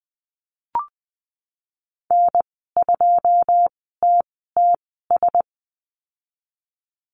Morse code, heard as a keyed tone, about 20 wpm, spaced as for 15 wpm, 700 Hz. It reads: N2TTS
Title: 088 Lesson 15g - Number 5 - Call Signs 20wpm